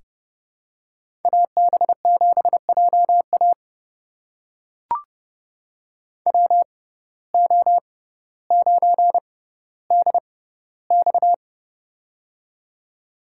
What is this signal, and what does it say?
Morse code, heard as a keyed tone, about 30 wpm, spaced as for 15 wpm, 700 Hz. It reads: AB7JA WO9DX
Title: USA Call Signs - 5 Characters - 6x Character Spacing - 30wpm